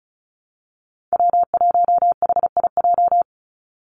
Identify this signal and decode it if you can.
Morse code, heard as a keyed tone, about 34 wpm, 700 Hz. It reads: W1HIJ
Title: Call Signs Repeated 3x - 35wpm